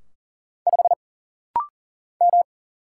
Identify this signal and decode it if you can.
Morse code, heard as a keyed tone, about 40 wpm, 700 Hz. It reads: 5 M
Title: Single Letter-Number Alt - Rapid-Fire 40wpm